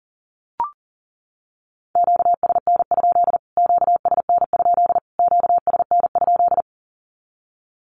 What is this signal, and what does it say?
Morse code, heard as a keyed tone, about 40 wpm, 700 Hz. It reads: QSN? QSN? QSN?
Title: Q-Codes Repeated 3x 40wpm